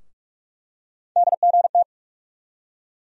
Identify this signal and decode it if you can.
Morse code, heard as a keyed tone, about 42 wpm, 700 Hz. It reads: DGT
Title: USA Call Sign Postfixes - 3 Letters 45wpm